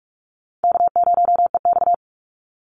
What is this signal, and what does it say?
Morse code, heard as a keyed tone, about 44 wpm, 700 Hz. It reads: K0EX
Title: Easy Call Signs - 45wpm